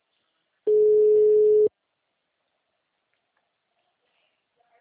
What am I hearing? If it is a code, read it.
Ringback tone (Europe, 425 Hz).
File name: out- -2043- - - .13093